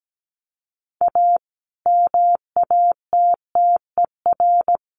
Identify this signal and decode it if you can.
Morse code, heard as a keyed tone, about 17 wpm, 700 Hz. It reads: A MATTER
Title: Sentences from Top 1100 Words 17wpm